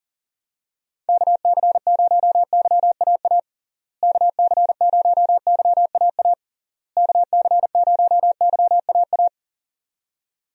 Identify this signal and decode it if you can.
Morse code, heard as a keyed tone, about 40 wpm, 700 Hz. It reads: KC0YAA KC0YAA KC0YAA
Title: USA Call Signs - 6 Characters - Repeated 3x - 40wpm